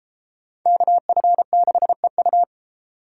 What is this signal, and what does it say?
Morse code, heard as a keyed tone, about 33 wpm, 700 Hz. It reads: KF6EU